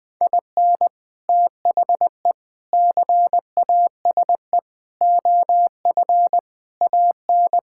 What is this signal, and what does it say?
Morse code, heard as a keyed tone, about 21 wpm, 700 Hz. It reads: IN THE CASE OF AN
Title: Sentences with 3-Word Phrases - Spoken Beginning - 20wpm